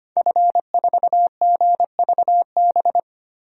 Morse code, heard as a keyed tone, about 25 wpm, 700 Hz. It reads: F4GVB
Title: DXCC-100 Call Signs - 5 Characters - Repeated 3x - 25wpm